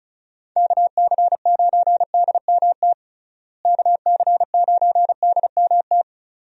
Morse code, heard as a keyed tone, about 36 wpm, 700 Hz. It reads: KC9DMT KC9DMT
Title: USA Call Signs - 6 Characters - Repeated 2x - 35wpm